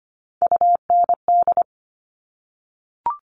Morse code, heard as a keyed tone, about 25 wpm, 700 Hz. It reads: UND
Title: Top Three Letter Combinations in Words 25wpm